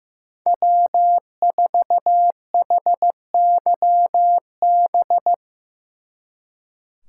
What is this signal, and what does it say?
Morse code, heard as a keyed tone, about 17 wpm, 700 Hz. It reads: W4HYB
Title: Speed Racing - USA 1x3 Call Signs 15wpm